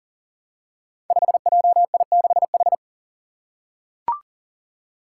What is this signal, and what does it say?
Morse code, heard as a keyed tone, about 40 wpm, 700 Hz. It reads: 5JI6H
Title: Five Challenging Letter-Number Combinations 40wpm